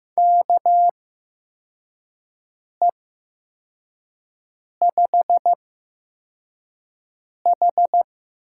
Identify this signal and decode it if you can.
Morse code, heard as a keyed tone, about 15 wpm, spaced as for 4 wpm, 700 Hz. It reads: KE5H
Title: USA Call Signs - 6 Characters - 8x Character Spacing - 15wpm